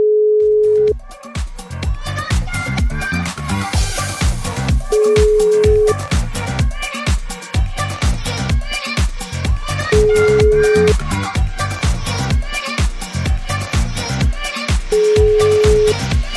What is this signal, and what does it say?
Ringback tone (Europe, 425 Hz), with noise.